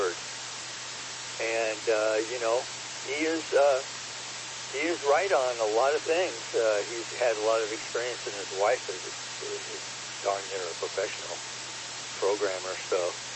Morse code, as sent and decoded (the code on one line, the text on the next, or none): none